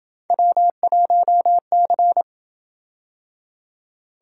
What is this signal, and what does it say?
Morse code, heard as a keyed tone, about 26 wpm, 700 Hz. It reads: W1C